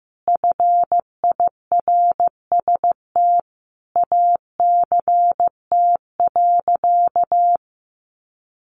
Morse code, heard as a keyed tone, about 15 wpm, 700 Hz. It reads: FIRST ACT.
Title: Sentences from Top 200 Words 15wpm